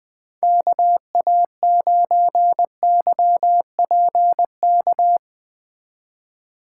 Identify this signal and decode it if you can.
Morse code, heard as a keyed tone, about 20 wpm, 700 Hz. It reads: KA9YPK
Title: Speed Racing - USA 2x3 Call Signs 17wpm